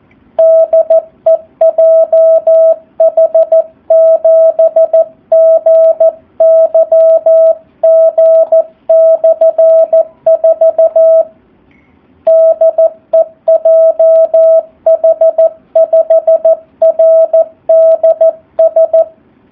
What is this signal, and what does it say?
Morse code, heard as a keyed tone, about 14 wpm, 650 Hz, with radio noise. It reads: DE JH7GYG/4 DE JH5RDS
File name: jh8cbh / 2019cw / SP04- w